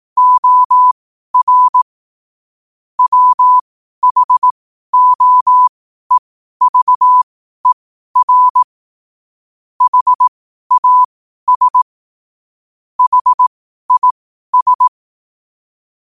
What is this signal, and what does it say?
Morse code, heard as a keyed tone, about 18 wpm, spaced as for 12 wpm, 1000 Hz. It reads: OR WHOEVER HAS HIS
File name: qotdfiles / 12wpm